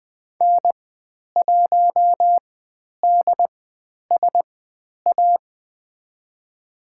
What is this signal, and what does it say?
Morse code, heard as a keyed tone, about 20 wpm, spaced as for 10 wpm, 700 Hz. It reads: N1DSA
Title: 095 Lesson 16g - Letter c - Call Signs 20wpm